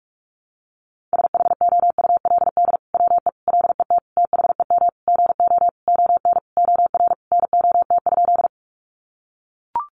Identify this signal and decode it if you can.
Morse code, heard as a keyed tone, about 44 wpm, 700 Hz. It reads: SHOULD WE LET THEM GO ON OR NOT?
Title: Sentences from Top 600 Words 45wpm